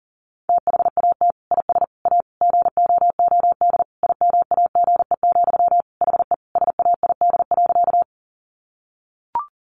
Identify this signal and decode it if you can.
Morse code, heard as a keyed tone, about 40 wpm, 700 Hz. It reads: THAT IS A GOOD IMAGE, HE SAID.
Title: Sentences from Top 700 Words 40wpm